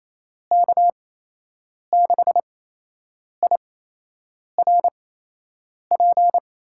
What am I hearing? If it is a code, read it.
Morse code, heard as a keyed tone, about 28 wpm, spaced as for 8 wpm, 700 Hz. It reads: K6IRP
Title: USA Call Signs - 8x Character Spacing - 28wpm